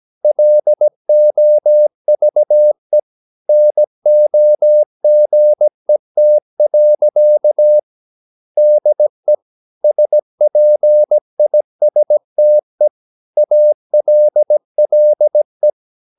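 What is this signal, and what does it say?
Morse code, heard as a keyed tone, about 17 wpm, 600 Hz. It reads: LOVE NOGET. DE SPISTE ALLE
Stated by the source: Slow (QRS) Morse practice at 17 wpm, from CW med Gnister.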